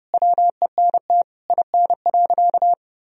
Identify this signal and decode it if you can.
Morse code, heard as a keyed tone, about 28 wpm, 700 Hz. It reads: WENT IN.
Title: Sentences from Top 300 Words and Abbreviations 30wpm